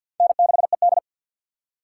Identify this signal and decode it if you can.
Morse code, heard as a keyed tone, about 50 wpm, 700 Hz. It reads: N6ED
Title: USA Call Signs - 4 Characters - 6x Character Spacing - 50wpm